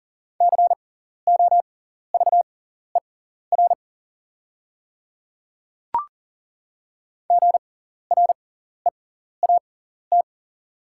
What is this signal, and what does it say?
Morse code, heard as a keyed tone, about 40 wpm, spaced as for 14 wpm, 690 Hz. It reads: COVER GREAT
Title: 6x Character Spacing - Top 300 Words - 40wpm